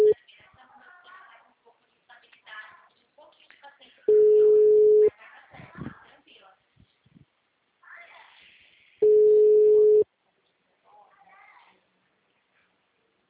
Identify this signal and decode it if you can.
Ringback tone (Europe, 425 Hz), with speech.
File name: out- -2012- - - .65800